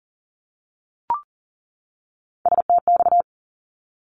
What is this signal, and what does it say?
Morse code, heard as a keyed tone, about 38 wpm, 700 Hz. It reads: STX
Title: ARRL Sections 40wpm